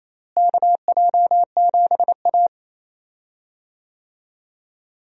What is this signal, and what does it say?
Morse code, heard as a keyed tone, about 28 wpm, 700 Hz. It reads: KJ7A
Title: USA Call Signs - 4 Characters - Repeated 3x - 28wpm